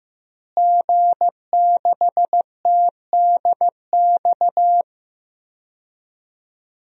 Morse code, heard as a keyed tone, about 15 wpm, 700 Hz. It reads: G6TDX
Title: DXCC-100 Call Signs - 5 Characters - Repeated 2x - 15wpm